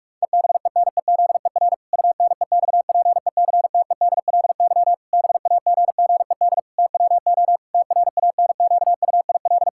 Morse code, heard as a keyed tone, about 46 wpm, 700 Hz. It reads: EBENEZER UNEXPECTEDLY BAGGED TWO TRANQUIL
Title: Pangram Challenge v2 - 45wpm